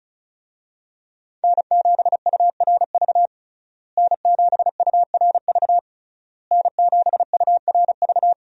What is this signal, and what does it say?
Morse code, heard as a keyed tone, about 35 wpm, 700 Hz. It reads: N7URV N7URV N7URV
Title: USA Call Signs - 5 Characters - Repeated 3x - 35wpm